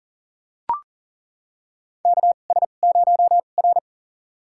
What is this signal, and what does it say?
Morse code, heard as a keyed tone, about 40 wpm, spaced as for 29 wpm, 700 Hz. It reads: KS0R
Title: USA Call Signs - 4 Characters - 2x Character Spacing - 40wpm